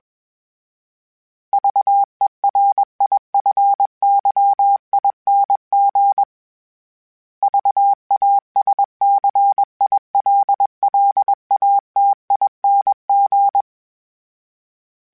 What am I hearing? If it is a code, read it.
Morse code, heard as a keyed tone, about 21 wpm, 800 Hz. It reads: VERIFYING VASCILLATING
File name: Special INGs / Special INGs-mp3s / Sngl Ing @20